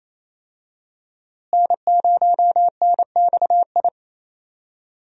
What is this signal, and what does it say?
Morse code, heard as a keyed tone, about 28 wpm, 700 Hz. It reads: N0NXI
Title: USA Call Signs - 5 Characters - Repeated 3x - 28wpm